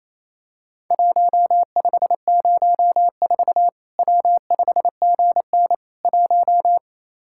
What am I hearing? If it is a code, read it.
Morse code, heard as a keyed tone, about 28 wpm, 700 Hz. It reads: 1504 W5GN 1
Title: Worked All Europe DX Contest - QTC Exchange - 28wpm